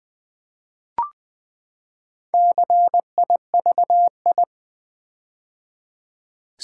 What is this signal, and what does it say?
Morse code, heard as a keyed tone, about 20 wpm, 700 Hz. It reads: CIVI
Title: Build A Word - 4 Letters At A Time - 12-Letter Words - 20wpm